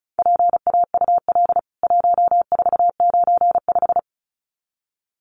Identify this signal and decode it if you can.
Morse code, heard as a keyed tone, about 34 wpm, 700 Hz. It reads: PAUL 1495
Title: CWT Exchange 35wpm